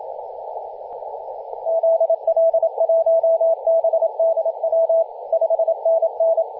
Morse code, heard as a keyed tone, about 27 wpm, 660 Hz, with radio noise. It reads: ZL1BDW 5NN
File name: e51wwa-10-104mhz-cw